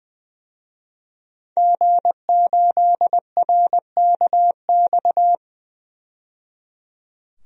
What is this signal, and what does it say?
Morse code, heard as a keyed tone, about 20 wpm, 700 Hz. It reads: G8RKX
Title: DXCC-100 Call Signs - 5 Characters - 20wpm